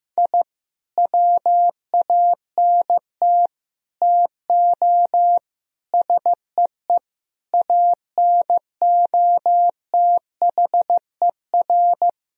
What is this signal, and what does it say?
Morse code, heard as a keyed tone, about 15 wpm, 700 Hz. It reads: I WANT TO SEE ANOTHER
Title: Sentences from Top 1000 Words - Review Part 1 - 15wpm